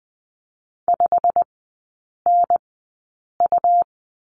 Morse code, heard as a keyed tone, about 20 wpm, spaced as for 8 wpm, 700 Hz. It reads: 5NU
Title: 107 Lesson 18e - Letter m - Three Characters 20wpm